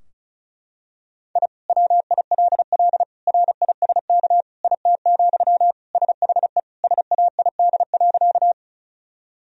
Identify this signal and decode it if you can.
Morse code, heard as a keyed tone, about 34 wpm, 700 Hz. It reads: I WILL RISK IT, SHE SAID.
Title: Sentences from Top 1000 Words - Review Part 4 - 35wpm